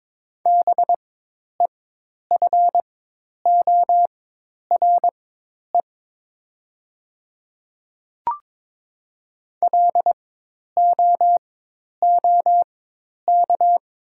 Morse code, heard as a keyed tone, about 22 wpm, spaced as for 10 wpm, 700 Hz. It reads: BEFORE LOOK
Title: 4x Character Spacing - Top 200 Words - 22wpm